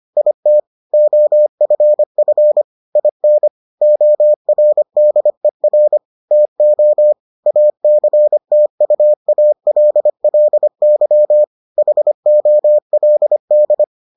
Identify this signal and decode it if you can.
Morse code, heard as a keyed tone, about 25 wpm, 600 Hz. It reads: IT OFF IN ORDER TO ACTUALLY HOLD